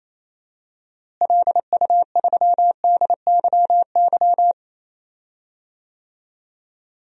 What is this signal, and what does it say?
Morse code, heard as a keyed tone, about 28 wpm, 700 Hz. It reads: LU3DYY
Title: DXCC-100 Call Signs - 6 Characters - Repeated 2x - 28wpm